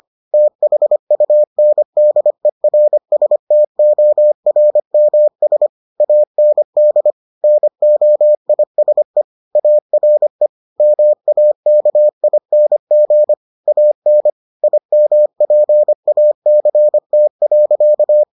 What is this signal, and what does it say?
Morse code, heard as a keyed tone, about 25 wpm, 600 Hz. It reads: THUNDERSTORMS AND NOISE ARE MAKING AN IMPACT.